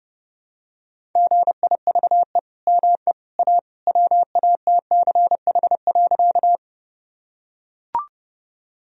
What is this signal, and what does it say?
Morse code, heard as a keyed tone, about 30 wpm, 700 Hz. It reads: GIVE ME A WATCH.
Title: Sentences from Top 1000 Words - Review Part 1 - 30wpm